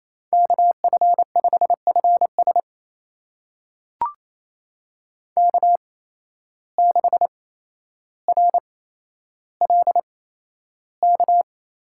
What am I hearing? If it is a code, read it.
Morse code, heard as a keyed tone, about 28 wpm, spaced as for 12 wpm, 700 Hz. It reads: KF5FS K6RLK
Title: USA Call Signs - 5 Characters - 8x Character Spacing - 28wpm